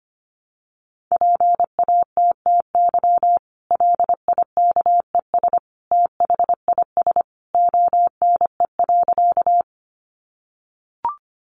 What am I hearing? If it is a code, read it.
Morse code, heard as a keyed tone, about 25 wpm, 700 Hz. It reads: PATTY LIKES THIS ONE.
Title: Additional Easy Phrases 25wpm